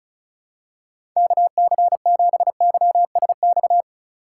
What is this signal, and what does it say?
Morse code, heard as a keyed tone, about 36 wpm, 700 Hz. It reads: KC7YSX
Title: USA Call Signs - 6 Characters - 4x Character Spacing - 35wpm